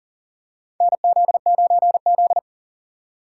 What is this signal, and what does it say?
Morse code, heard as a keyed tone, about 40 wpm, 700 Hz. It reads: NZ9Z
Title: USA Call Signs - 4 Characters - 4x Character Spacing - 40wpm